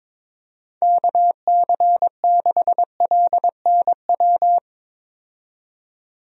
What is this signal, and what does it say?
Morse code, heard as a keyed tone, about 22 wpm, 700 Hz. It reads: KC6LNW
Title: USA Call Signs - 6 Characters - 22wpm